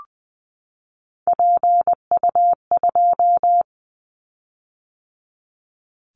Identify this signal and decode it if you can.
Morse code, heard as a keyed tone, about 20 wpm, 700 Hz. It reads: PU2
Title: Callsign Prefixes for Top 50 DXCC - 20wpm